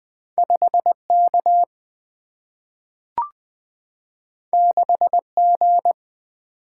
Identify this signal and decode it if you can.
Morse code, heard as a keyed tone, about 20 wpm, 700 Hz. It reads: 5K 6G
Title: Single-Letter Single-Number 20wpm